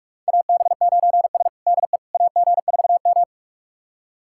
Morse code, heard as a keyed tone, about 46 wpm, 700 Hz. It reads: AB9S DE AG4M
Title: USA Callsign DE Callsign - 4 Characters - 45wpm